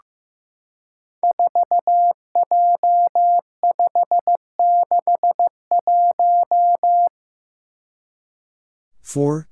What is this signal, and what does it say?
Morse code, heard as a keyed tone, about 15 wpm, 700 Hz. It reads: 4J561
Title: Five Challenging Letter-Number Combinations 15wpm